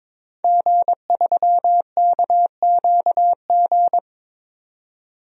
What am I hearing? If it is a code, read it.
Morse code, heard as a keyed tone, about 22 wpm, 700 Hz. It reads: G3KQG
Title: DXCC-100 Call Signs - 5 Characters - 22wpm